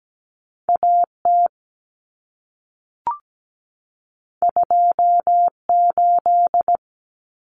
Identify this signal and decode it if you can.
Morse code, heard as a keyed tone, about 17 wpm, 700 Hz. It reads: AT 28
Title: Common Cut Numbers - 2-Digit 17wpm